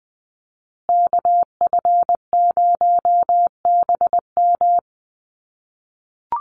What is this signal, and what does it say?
Morse code, heard as a keyed tone, about 20 wpm, 700 Hz. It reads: KF0BM
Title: USA Call Signs Repeated 2x 20wpm